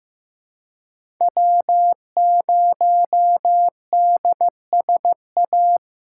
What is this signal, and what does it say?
Morse code, heard as a keyed tone, about 15 wpm, 700 Hz. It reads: W0DSA
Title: USA Call Signs - 4x Character Spacing - 15wpm